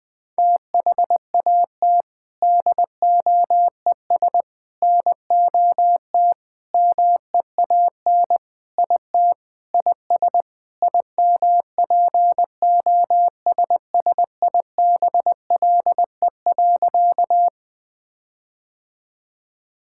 Morse code, heard as a keyed tone, about 20 wpm, 700 Hz. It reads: THAT DOES NOT MEAN IT IS IMPOSSIBLE.
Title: Sentences with 4-Word Phrases - Spoken Beginning - 20wpm